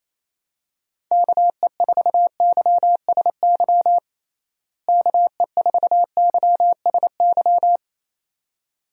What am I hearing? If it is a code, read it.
Morse code, heard as a keyed tone, about 27 wpm, 700 Hz. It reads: KE4YSY KE4YSY
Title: USA Call Signs - 6 Characters - Repeated 2x - 28wpm